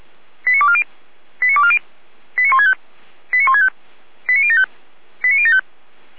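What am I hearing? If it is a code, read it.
Five-tone selective call (ZVEI), heard as quick runs of five tones, tones of about 70 ms. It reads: 89240 89240 89165 89165 89075 89075
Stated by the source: ZVEI five-tone selective call received on SDR.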